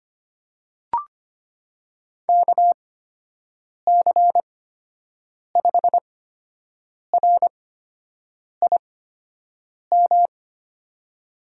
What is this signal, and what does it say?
Morse code, heard as a keyed tone, about 25 wpm, spaced as for 7 wpm, 700 Hz. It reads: KC5RIM
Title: USA Call Signs - 6 Characters - 8x Character Spacing - 25wpm